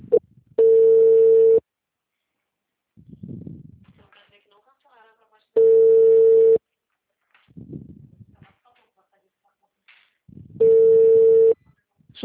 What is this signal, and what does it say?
Ringback tone (China, 450 Hz), with speech.
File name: out- -2008